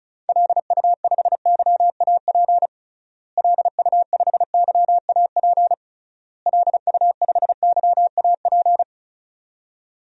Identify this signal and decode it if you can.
Morse code, heard as a keyed tone, about 34 wpm, 700 Hz. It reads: LU5YAP LU5YAP LU5YAP
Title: DXCC-100 Call Signs - 6 Characters - Repeated 3x - 35wpm